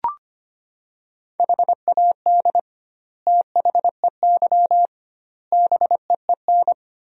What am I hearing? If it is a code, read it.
Morse code, heard as a keyed tone, about 25 wpm, 700 Hz. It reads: HAD THEY BEEN
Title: 2x Word-Spacing - Sentences from Top 400 Words - 25wpm